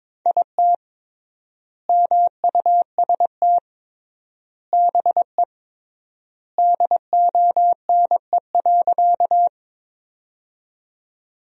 Morse code, heard as a keyed tone, about 22 wpm, 700 Hz. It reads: IT MUST BE DONE.
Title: 3x Word-Spacing - Sentences from Top 500 Words - 22wpm